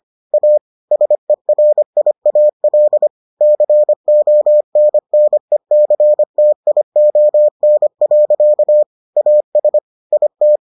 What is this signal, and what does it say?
Morse code, heard as a keyed tone, about 25 wpm, 600 Hz. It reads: A SERIAL CONNECTION. AS IT